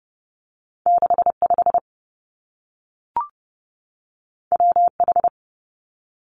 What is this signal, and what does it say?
Morse code, heard as a keyed tone, about 30 wpm, 700 Hz. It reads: 6 5 W H
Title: Two Challenging Letter-Number Combinations 30wpm